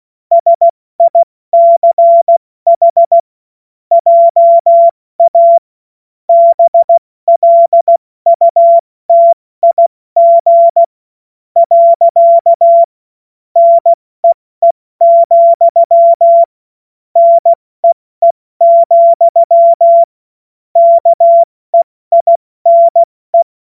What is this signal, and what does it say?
Morse code, heard as a keyed tone, about 16 wpm, spaced as for 14 wpm, 680 Hz. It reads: SICH JA BLUTIG. NEE, NEE, KEINE
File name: Morsekurs / Witze / Witz 07. Februar 2023 70 F80